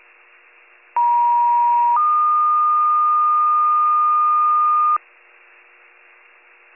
Two-tone page, heard as a one-second tone, then a three-second tone, about 950 then 1220 Hz. It is heard from 1.0 to 5.0 s.